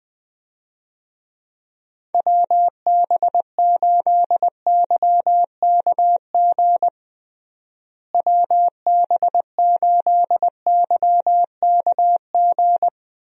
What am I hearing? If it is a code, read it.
Morse code, heard as a keyed tone, about 20 wpm, 700 Hz. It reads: WB8YKG WB8YKG
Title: USA Call Signs - 6 Characters - Repeated 2x - 20wpm